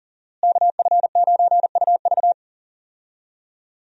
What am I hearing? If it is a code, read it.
Morse code, heard as a keyed tone, about 40 wpm, 700 Hz. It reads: KF9UV
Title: USA Call Signs - 5 Characters - Repeated 3x - 40wpm